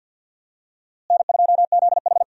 Morse code, heard as a keyed tone, about 47 wpm, 700 Hz. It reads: N2ZH
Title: USA Call Signs - 4 Characters - Repeated 3x - 50wpm